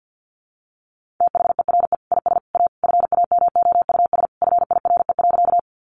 Morse code, heard as a keyed tone, about 50 wpm, 700 Hz. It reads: THERE IS A FAMOUS FIRE.
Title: Sentences from Top 1500 Words 50wpm